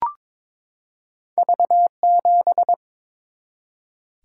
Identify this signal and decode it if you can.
Morse code, heard as a keyed tone, about 23 wpm, 700 Hz. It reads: V7
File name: practice / 22wpm / Two Letter-Number Combinations - 22wpm